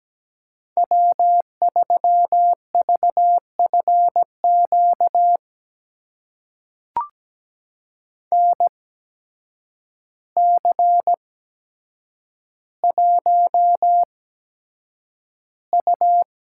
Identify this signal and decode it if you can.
Morse code, heard as a keyed tone, about 17 wpm, spaced as for 8 wpm, 700 Hz. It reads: W3VFQ NC1U
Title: USA Call Signs - 8x Character Spacing - 17wpm